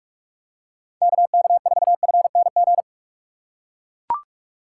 Morse code, heard as a keyed tone, about 44 wpm, 700 Hz. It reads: KK4FNG
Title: USA Call Signs - 6 Characters - 6x Character Spacing - 45wpm